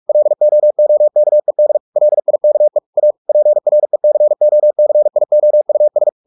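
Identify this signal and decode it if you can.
Morse code, heard as a keyed tone, about 44 wpm, 600 Hz. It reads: LOOKED LIKE A PRECOCIOUS
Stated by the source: Morse practice at 45 wpm, the novel Never by Ken Follett in English, from CW med Gnister.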